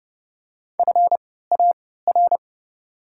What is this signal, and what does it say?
Morse code, heard as a keyed tone, about 30 wpm, spaced as for 17 wpm, 700 Hz. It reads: FAR
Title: 3x Character Spacing - Top 500 Words - 30wpm